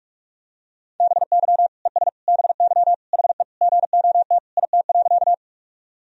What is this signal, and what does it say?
Morse code, heard as a keyed tone, about 41 wpm, 700 Hz. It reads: BY ES BY HE GOT IT.
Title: Sentences from Top 300 Words and Abbreviations 45wpm